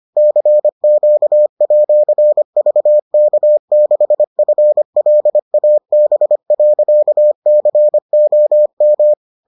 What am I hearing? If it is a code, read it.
Morse code, heard as a keyed tone, about 25 wpm, 600 Hz. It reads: CQ@VK6FLAB.COM